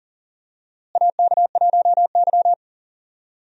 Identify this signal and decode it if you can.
Morse code, heard as a keyed tone, about 40 wpm, 700 Hz. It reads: AK1Y
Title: USA Call Signs - 4 Characters - Repeated 3x - 40wpm